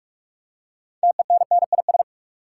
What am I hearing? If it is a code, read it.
Morse code, heard as a keyed tone, about 46 wpm, 700 Hz. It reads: TENNIS
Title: Six Letter Words - 45wpm